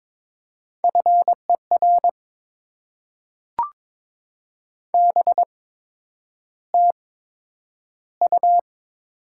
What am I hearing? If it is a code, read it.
Morse code, heard as a keyed tone, about 22 wpm, spaced as for 9 wpm, 700 Hz. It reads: FER BTU
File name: Top 20 QSO Elements - 8x Character Spacing - 22wpm